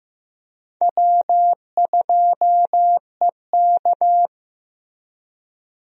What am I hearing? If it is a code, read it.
Morse code, heard as a keyed tone, about 15 wpm, 700 Hz. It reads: W2EK
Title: USA Call Signs - 4 Characters - Repeated 2x - 15wpm